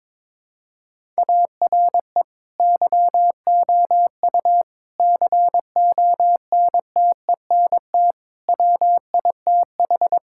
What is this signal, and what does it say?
Morse code, heard as a keyed tone, about 22 wpm, 700 Hz. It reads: ARE YOU CONTENT WITH
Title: Sentences from Top 1000 Words - Review Part 6 - 22wpm